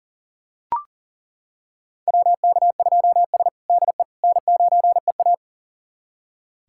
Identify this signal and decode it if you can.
Morse code, heard as a keyed tone, about 37 wpm, 700 Hz. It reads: WK2S DE N9EA
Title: USA Callsign DE Callsign - 4 Characters - 40wpm